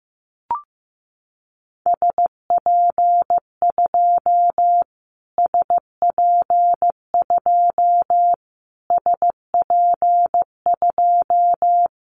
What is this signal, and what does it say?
Morse code, heard as a keyed tone, about 15 wpm, 700 Hz. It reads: SP2 SP2 SP2
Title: Callsign Prefixes for Top 25 DXCC - Repeated 3x - 15wpm